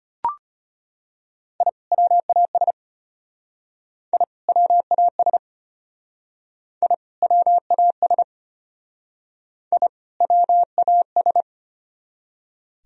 Morse code, heard as a keyed tone, about 33 wpm, 700 Hz. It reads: I WAS I WAS I WAS I WAS